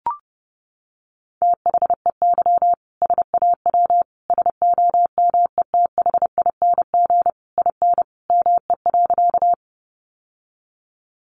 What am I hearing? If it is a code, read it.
Morse code, heard as a keyed tone, about 30 wpm, 700 Hz. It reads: THEY SAW SOMETHING IN ME.
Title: Sentences from Top 600 Words 30wpm